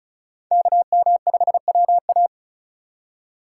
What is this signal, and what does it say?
Morse code, heard as a keyed tone, about 34 wpm, 700 Hz. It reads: KM5WA
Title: USA Call Signs - 5 Characters - 4x Character Spacing - 35wpm